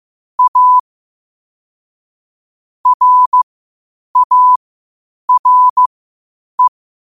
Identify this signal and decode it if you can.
Morse code, heard as a keyed tone, about 15 wpm, spaced as for 8 wpm, 1000 Hz. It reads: A RARE
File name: qotdfiles / 8wpm